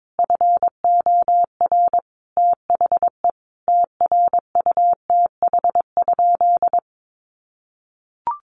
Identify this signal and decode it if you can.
Morse code, heard as a keyed tone, about 22 wpm, 700 Hz. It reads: FOR THE TRUTH?
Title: Sentences from Top 1300 Words 22wpm